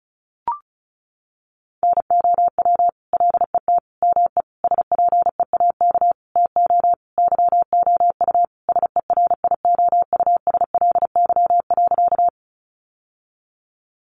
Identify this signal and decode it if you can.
Morse code, heard as a keyed tone, about 34 wpm, 700 Hz. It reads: NOW LET ME SPEAK TO YOU SERIOUSLY.
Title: Sentences from Top 1200 Words 35wpm